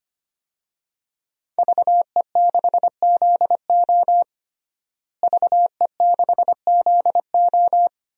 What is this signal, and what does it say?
Morse code, heard as a keyed tone, about 25 wpm, 700 Hz. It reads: VE6ZO VE6ZO
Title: DXCC-100 Call Signs - 5 Characters - Repeated 2x - 25wpm